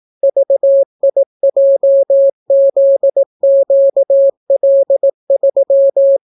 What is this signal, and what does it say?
Morse code, heard as a keyed tone, about 18 wpm, 550 Hz. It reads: VIJZQL3